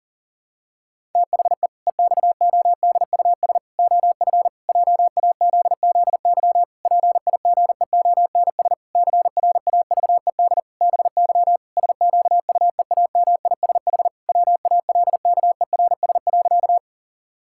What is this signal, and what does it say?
Morse code, heard as a keyed tone, about 37 wpm, 700 Hz. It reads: THE EXODUS OF JAZZY PIGEONS CRAVED BY SQUEAMISH WALKERS.
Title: Pangram Challenge v2 - 40wpm